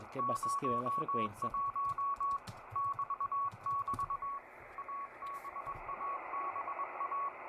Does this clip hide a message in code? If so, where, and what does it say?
Yes: Morse code from 0.2 to 7.3 s, heard as a keyed tone, about 21 wpm, 1120 Hz, with speech and radio noise. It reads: 01 <BT><BT> ABABA